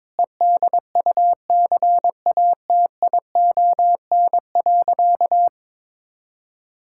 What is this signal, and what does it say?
Morse code, heard as a keyed tone, about 22 wpm, 700 Hz. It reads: EDUCATION.
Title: Sentences from Top 1000 Words - Review Part 6 - 22wpm